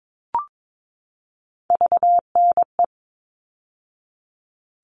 Morse code, heard as a keyed tone, about 22 wpm, 700 Hz. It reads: VNE
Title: All Cut Numbers - 3-Digit 22wpm